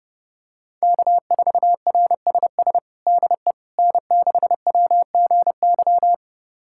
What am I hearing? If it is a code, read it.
Morse code, heard as a keyed tone, about 30 wpm, 700 Hz. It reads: K4RSS DE N6WGY
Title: USA Callsign DE Callsign 30wpm